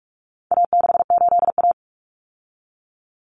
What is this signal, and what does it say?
Morse code, heard as a keyed tone, about 44 wpm, 700 Hz. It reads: A68A
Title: Common Cut Numbers - 4-Digit 45wpm